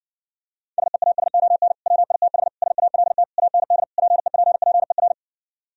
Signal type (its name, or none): Morse code